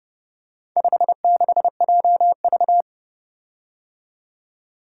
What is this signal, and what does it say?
Morse code, heard as a keyed tone, about 31 wpm, 700 Hz. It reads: 56JV